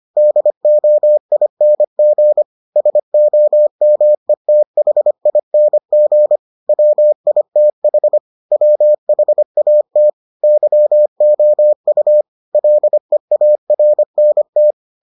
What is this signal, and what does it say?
Morse code, heard as a keyed tone, about 25 wpm, 600 Hz. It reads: DOING SOMETHING WITH WHAT YOU LEARNT